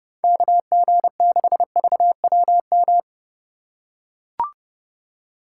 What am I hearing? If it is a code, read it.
Morse code, heard as a keyed tone, about 30 wpm, 700 Hz. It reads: KG6VWM
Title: USA Call Signs - 6 Characters - 3x Character Spacing - 30wpm